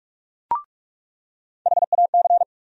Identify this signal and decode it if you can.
Morse code, heard as a keyed tone, about 42 wpm, 700 Hz. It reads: HAC
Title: USA Call Sign Postfixes - 3 Letters 45wpm